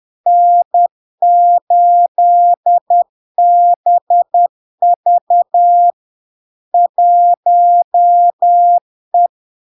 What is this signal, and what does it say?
Morse code, heard as a keyed tone, about 10 wpm, 700 Hz. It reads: N8BV 1E
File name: LARK-FD-2022-Log-in-CW-10wpm-700-Hz